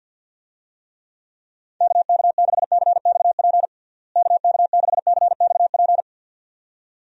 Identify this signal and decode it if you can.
Morse code, heard as a keyed tone, about 48 wpm, 700 Hz. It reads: KK6CXP KK6CXP
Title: USA Call Signs - 6 Characters - Repeated 2x - 50wpm